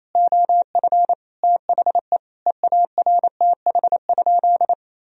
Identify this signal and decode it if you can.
Morse code, heard as a keyed tone, about 28 wpm, 700 Hz. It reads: OF THE EARTH?
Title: Sentences from Top 1000 Words - Review Part 3 - 28wpm